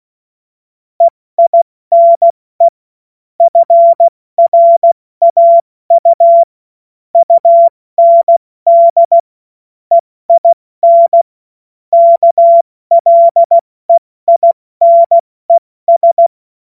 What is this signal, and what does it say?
Morse code, heard as a keyed tone, about 16 wpm, 680 Hz. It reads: EINE FRAU UND EIN KLEINES
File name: Morsekurs / Witze / Witz 23. August 2022 70 F80